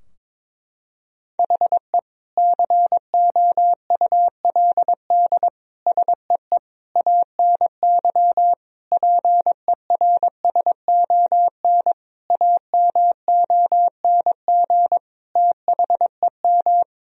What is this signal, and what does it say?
Morse code, heard as a keyed tone, about 22 wpm, 700 Hz. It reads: HE COULD SEE ANY PERSON AMONG THEM
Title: Sentences from Top 1000 Words - Review Part 3 - 22wpm